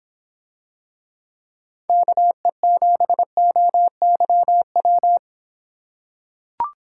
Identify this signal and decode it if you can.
Morse code, heard as a keyed tone, about 26 wpm, 700 Hz. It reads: KE7OYW